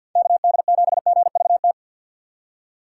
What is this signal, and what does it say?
Morse code, heard as a keyed tone, about 50 wpm, 700 Hz. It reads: KD7GVT
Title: USA Call Signs - 6 Characters - 6x Character Spacing - 50wpm